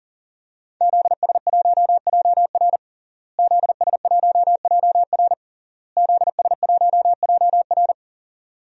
Morse code, heard as a keyed tone, about 37 wpm, 700 Hz. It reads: ZS1JR ZS1JR ZS1JR
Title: DXCC-100 Call Signs - 5 Characters - Repeated 3x - 40wpm